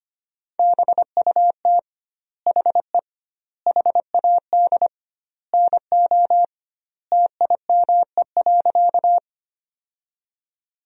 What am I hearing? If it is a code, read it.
Morse code, heard as a keyed tone, about 25 wpm, 700 Hz. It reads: BUT HE HAD NO TIME.
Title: 2x Word-Spacing - Sentences from Top 100 Words - 25wpm